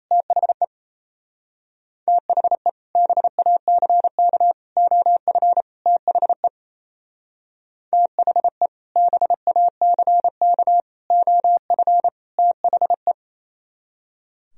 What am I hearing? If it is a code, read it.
Morse code, heard as a keyed tone, about 32 wpm, 700 Hz. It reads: THE THE BACK OF THE THE BACK OF THE
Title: Speed-Racing - Sets of 4 Words 28wpm